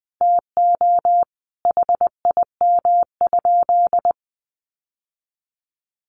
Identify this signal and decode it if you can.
Morse code, heard as a keyed tone, about 20 wpm, 700 Hz. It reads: TO HIM?
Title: Sentences from Top 1000 Words 20wpm